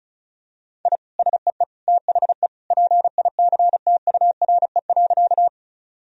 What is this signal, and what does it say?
Morse code, heard as a keyed tone, about 36 wpm, 700 Hz. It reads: I SEE THE PICTURE.
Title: Sentences from Top 1000 Words - Review Part 1 - 35wpm